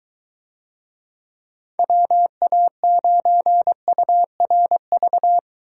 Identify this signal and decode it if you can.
Morse code, heard as a keyed tone, about 22 wpm, 700 Hz. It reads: WA9URV